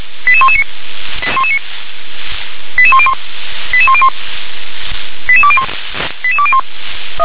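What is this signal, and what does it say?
Five-tone selective call (ZVEI), heard as quick runs of five tones, tones of about 70 ms. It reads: 88119 88101 88101 88201 88201
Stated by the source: Five-tone ZVEI selective call received by SDR.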